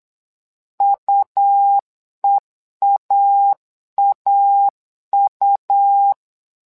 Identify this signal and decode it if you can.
Morse code, heard as a keyed tone, about 8 wpm, 800 Hz. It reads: UEAAU
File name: LEZIONI / 2 UHDN / 2 U 8wpm 2028